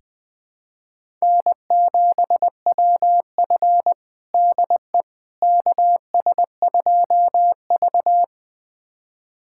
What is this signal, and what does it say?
Morse code, heard as a keyed tone, about 20 wpm, 700 Hz. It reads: N7WF DE KS2V
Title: USA Callsign DE Callsign - 4 Characters - 20wpm